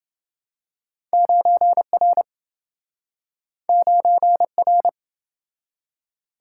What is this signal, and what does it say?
Morse code, heard as a keyed tone, about 27 wpm, 700 Hz. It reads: 9R 9R